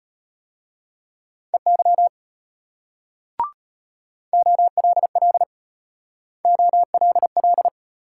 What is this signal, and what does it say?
Morse code, heard as a keyed tone, about 32 wpm, 700 Hz. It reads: EY OLL OLL